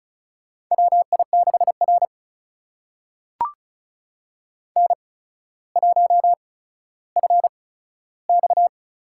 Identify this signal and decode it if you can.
Morse code, heard as a keyed tone, about 36 wpm, spaced as for 15 wpm, 700 Hz. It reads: WI6R N1FX
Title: USA Call Signs - 4 Characters - 8x Character Spacing - 35wpm